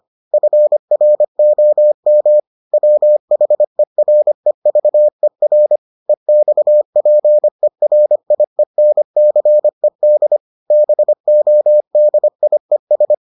Morse code, heard as a keyed tone, about 25 wpm, 600 Hz. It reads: FROM WHEREVER EXPERIENCED BODIES